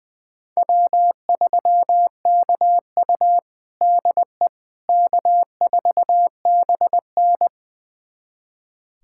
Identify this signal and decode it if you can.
Morse code, heard as a keyed tone, about 20 wpm, 700 Hz. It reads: W3KU DE K4BN
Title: USA Callsign DE Callsign - 4 Characters - 20wpm